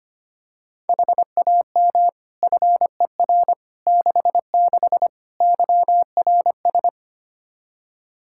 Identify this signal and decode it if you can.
Morse code, heard as a keyed tone, about 25 wpm, 700 Hz. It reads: HAM FER 66 YRS
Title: Practice QSO - Exchange 3 - More Info - 25wpm